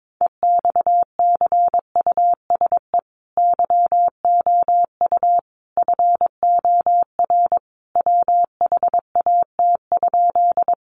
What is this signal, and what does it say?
Morse code, heard as a keyed tone, about 22 wpm, 700 Hz. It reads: EXCUSE YOU FOR WHAT?
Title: Sentences from Top 700 Words 22wpm